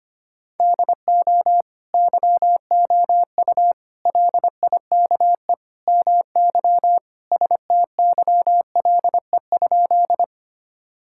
Morse code, heard as a keyed tone, about 25 wpm, 700 Hz. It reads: DO YOU LIKE MY STYLE?
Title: Sentences from Top 1000 Words - Review Part 5 - 25wpm